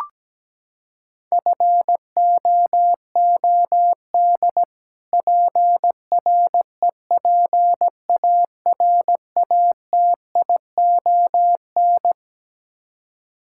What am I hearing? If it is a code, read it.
Morse code, heard as a keyed tone, about 17 wpm, 700 Hz. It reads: FOOD PREPARATION
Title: Practice QSO - Exchange 3 - More Info - 17wpm